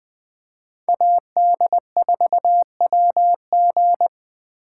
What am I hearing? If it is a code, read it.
Morse code, heard as a keyed tone, about 21 wpm, 700 Hz. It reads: AD4WG